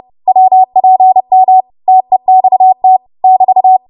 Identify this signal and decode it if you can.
Morse code, heard as a keyed tone, about 30 wpm, 750 Hz. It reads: WPM TEXT <BT>
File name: Archive / 30 WPM / 30WPM